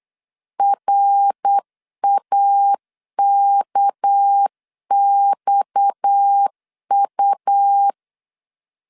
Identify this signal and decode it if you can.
Morse code, heard as a keyed tone, about 8 wpm, 800 Hz. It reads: RAKXU
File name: LEZIONI / 6 PKX0 / 6 PKX0 8wpm 3341